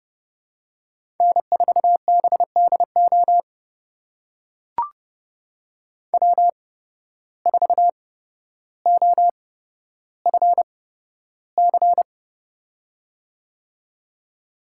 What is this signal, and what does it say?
Morse code, heard as a keyed tone, about 29 wpm, spaced as for 13 wpm, 700 Hz. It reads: N4BDO W4OFC
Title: USA Call Signs - 5 Characters - 8x Character Spacing - 30wpm